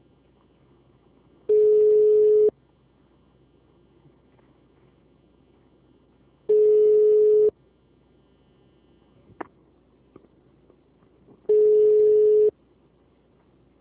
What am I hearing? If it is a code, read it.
Ringback tone (Europe, 425 Hz).
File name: out- -2006- - - .36